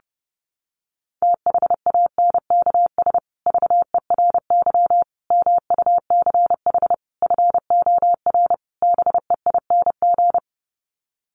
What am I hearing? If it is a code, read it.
Morse code, heard as a keyed tone, about 30 wpm, 700 Hz. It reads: THANKS VERY MUCH FOR BEING
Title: Sets of 5 Words Encore 30wpm